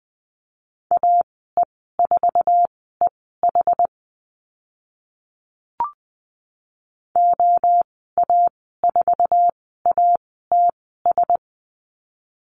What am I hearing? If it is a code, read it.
Morse code, heard as a keyed tone, about 20 wpm, spaced as for 15 wpm, 700 Hz. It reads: AE4EH OA4ATS
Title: 062 Lesson 11g - Letter h - Call Signs 20wpm